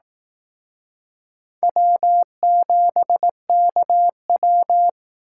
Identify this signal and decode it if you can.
Morse code, heard as a keyed tone, about 18 wpm, 700 Hz. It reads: W7KW
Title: Speed Racing - USA 1x2 Call Signs 15wpm